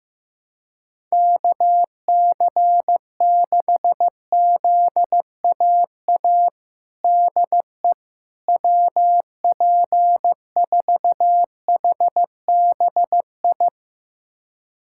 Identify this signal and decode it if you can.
Morse code, heard as a keyed tone, about 15 wpm, 700 Hz. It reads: KC6ZAA DE WP4HBI
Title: USA Callsign DE Callsign - 6 Characters - 15wpm